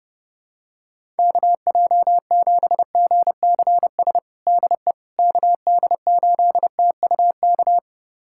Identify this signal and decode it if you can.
Morse code, heard as a keyed tone, about 30 wpm, 700 Hz. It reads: KJ7GCS DE KD8TUK
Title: USA Callsign DE Callsign 30wpm